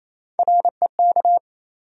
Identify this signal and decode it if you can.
Morse code, heard as a keyed tone, about 28 wpm, 700 Hz. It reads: REK
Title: USA Call Sign Postfixes - 3 Letters 28wpm